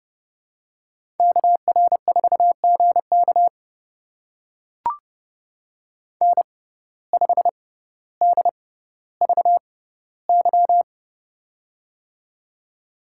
Morse code, heard as a keyed tone, about 29 wpm, spaced as for 15 wpm, 700 Hz. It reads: KR4GK N5DVY
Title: USA Call Signs - 5 Characters - 6x Character Spacing - 30wpm